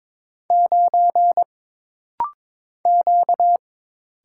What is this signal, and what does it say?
Morse code, heard as a keyed tone, about 22 wpm, 700 Hz. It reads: 9 Q
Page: This rapid-fire timing is geared to push your ICR (Instant Character Recognition) skills.